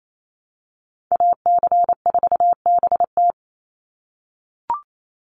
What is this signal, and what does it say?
Morse code, heard as a keyed tone, about 28 wpm, 700 Hz. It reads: AC4BT
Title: Call Signs - 28wpm